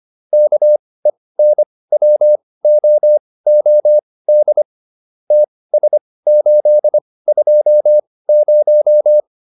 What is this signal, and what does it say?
Morse code, heard as a keyed tone, about 25 wpm, spaced as for 18 wpm, 600 Hz. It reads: KENWOOD TS820
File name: QSO408 18